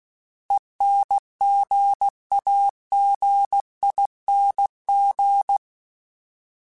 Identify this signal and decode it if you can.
Morse code, heard as a keyed tone, about 16 wpm, 800 Hz. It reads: ENGAGING
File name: Alphabetical / Alphabetical mp3s / Es @15